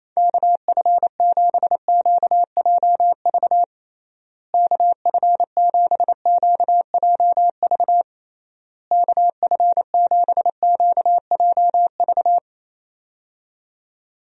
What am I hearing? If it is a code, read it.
Morse code, heard as a keyed tone, about 28 wpm, 700 Hz. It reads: KF7QJV KF7QJV KF7QJV
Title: USA Call Signs - 6 Characters - Repeated 3x - 28wpm